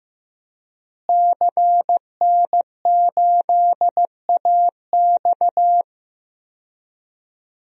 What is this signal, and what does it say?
Morse code, heard as a keyed tone, about 16 wpm, 700 Hz. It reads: CN8AX
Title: DXCC-100 Call Signs - 5 Characters - Repeated 2x - 15wpm